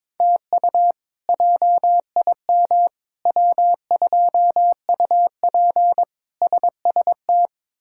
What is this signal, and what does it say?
Morse code, heard as a keyed tone, about 21 wpm, 700 Hz. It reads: TU JIM W2UP SST
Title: K1USN SST QSO Practice - 22wpm